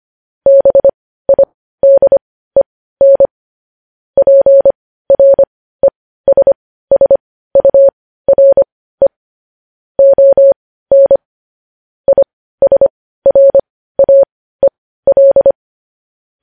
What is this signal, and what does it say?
Morse code, heard as a keyed tone, about 25 wpm, spaced as for 15 wpm, 550 Hz. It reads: BIDEN PRESSURE ON ISRAEL
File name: podcast / News headlines-1x-25x15